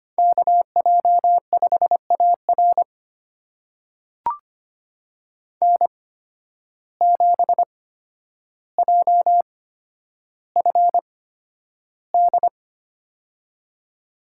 Morse code, heard as a keyed tone, about 25 wpm, spaced as for 11 wpm, 700 Hz. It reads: KJ5AR N7JFD
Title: USA Call Signs - 5 Characters - 8x Character Spacing - 25wpm